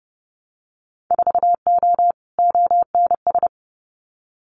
Morse code, heard as a keyed tone, about 30 wpm, 700 Hz. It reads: 4O ONS
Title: Winter Field Day Exchanges 30wpm